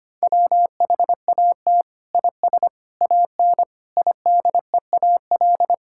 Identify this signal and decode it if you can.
Morse code, heard as a keyed tone, about 25 wpm, 700 Hz. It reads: WHAT IS AN IDEAL
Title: Sentences from Top 1600 Words 25wpm